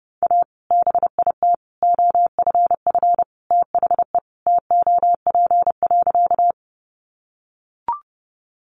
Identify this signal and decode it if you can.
Morse code, heard as a keyed tone, about 30 wpm, 700 Hz. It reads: A BIT OFF THE TOP.
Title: Sentences from Top 900 Words 30wpm